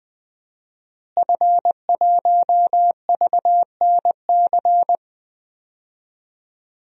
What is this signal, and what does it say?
Morse code, heard as a keyed tone, about 20 wpm, 700 Hz. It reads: F1VNC
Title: DXCC-100 Call Signs - 5 Characters - Repeated 3x - 20wpm